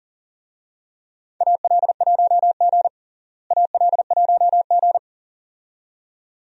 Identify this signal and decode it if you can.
Morse code, heard as a keyed tone, about 40 wpm, 700 Hz. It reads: AL1G AL1G
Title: USA Call Signs - 4 Characters - Repeated 2x - 40wpm